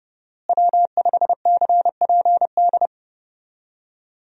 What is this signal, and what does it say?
Morse code, heard as a keyed tone, about 29 wpm, 700 Hz. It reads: W5CPD
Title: USA Call Signs - 5 Characters - Repeated 3x - 30wpm